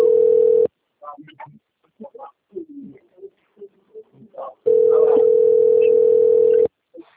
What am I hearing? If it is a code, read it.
Ringback tone (North America), 440 + 480 Hz.
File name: out- - - - - .55848